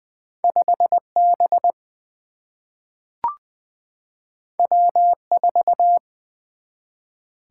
Morse code, heard as a keyed tone, about 20 wpm, 700 Hz. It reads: 5 B W 4
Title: Single-Letter Single-Number 20wpm